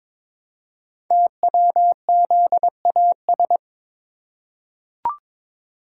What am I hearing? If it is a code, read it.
Morse code, heard as a keyed tone, about 22 wpm, 700 Hz. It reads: TWZAS
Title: Code Groups - 5 Letters - 22wpm